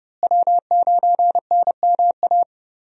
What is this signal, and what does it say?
Morse code, heard as a keyed tone, about 29 wpm, 700 Hz. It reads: W9NMA
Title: USA Call Signs - 5 Characters - Repeated 2x - 30wpm